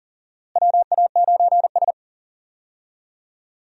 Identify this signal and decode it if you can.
Morse code, heard as a keyed tone, about 40 wpm, 700 Hz. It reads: WA9S
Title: USA Call Signs - 4 Characters - Repeated 3x - 40wpm